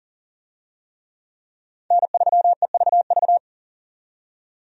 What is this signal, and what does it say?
Morse code, heard as a keyed tone, about 40 wpm, 700 Hz. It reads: N3EVV
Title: USA Call Signs - 5 Characters - 40wpm